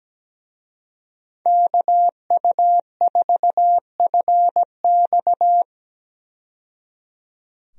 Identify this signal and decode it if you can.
Morse code, heard as a keyed tone, about 17 wpm, 700 Hz. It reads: KU4FX
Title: USA Call Signs - 5 Characters - 17wpm